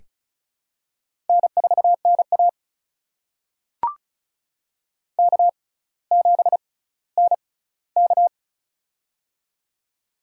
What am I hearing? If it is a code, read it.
Morse code, heard as a keyed tone, about 36 wpm, spaced as for 18 wpm, 700 Hz. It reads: N4NA K7NK
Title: USA Call Signs - 4 Characters - 6x Character Spacing - 35wpm